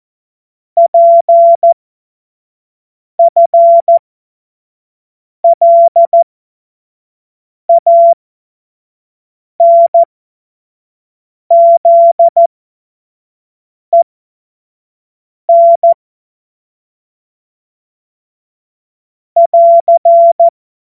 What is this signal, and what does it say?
Morse code, heard as a keyed tone, about 14 wpm, spaced as for 5 wpm, 680 Hz. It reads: PFLANZEN <AR>
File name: Morsekurs / Witze / Witz 27. Januar 2026 25 F70